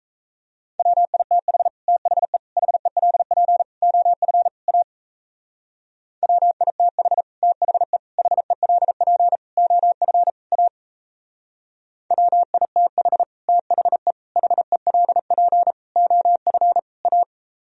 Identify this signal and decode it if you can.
Morse code, heard as a keyed tone, about 35 wpm, 700 Hz. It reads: WITH THE HELP OF A WITH THE HELP OF A WITH THE HELP OF A
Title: Speed-Racing - Sets of 5 Words 28wpm